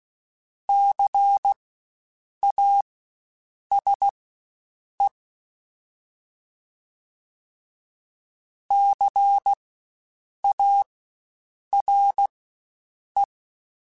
Morse code, heard as a keyed tone, about 16 wpm, spaced as for 7 wpm, 800 Hz. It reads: CASE CARE
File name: Spaced Alphas / Spaced Alphas mp3s / Cs sp @15